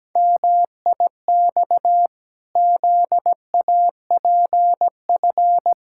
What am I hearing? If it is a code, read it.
Morse code, heard as a keyed tone, about 17 wpm, 700 Hz. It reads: MIX ZAPF
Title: Pangram Challenge v2 - 17wpm